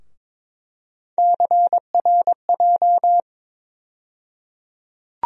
Morse code, heard as a keyed tone, about 22 wpm, 700 Hz. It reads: CRJ
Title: USA Call Sign Postfixes - 3 Letters 22wpm